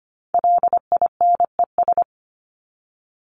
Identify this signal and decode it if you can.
Morse code, heard as a keyed tone, about 25 wpm, 700 Hz. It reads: LINES
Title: ES:S Suffix - Top 100 Words with Base Words 25wpm